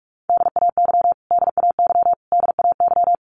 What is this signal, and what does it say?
Morse code, heard as a keyed tone, about 44 wpm, 700 Hz. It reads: DAY DAY DAY
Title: Top 500 Words Repeated 45wpm